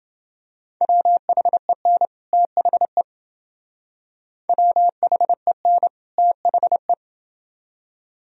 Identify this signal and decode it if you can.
Morse code, heard as a keyed tone, about 28 wpm, 700 Hz. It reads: WHEN THE WHEN THE